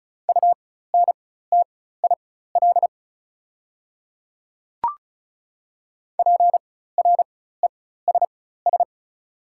Morse code, heard as a keyed tone, about 36 wpm, spaced as for 16 wpm, 700 Hz. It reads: UNTIL PRESS
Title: 4x Character Spacing - Top 300 Words - 35wpm